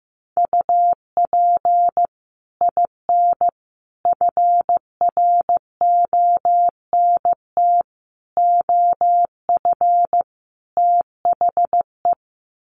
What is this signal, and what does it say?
Morse code, heard as a keyed tone, about 15 wpm, 700 Hz. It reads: UP IN FRONT OF THE
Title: Sets of 5 Words Encore2 15wpm